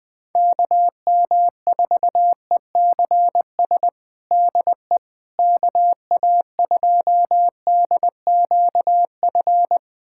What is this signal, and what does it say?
Morse code, heard as a keyed tone, about 21 wpm, 700 Hz. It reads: KM4ECS DE KA2DQF
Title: USA Callsign DE Callsign - 6 Characters - 20wpm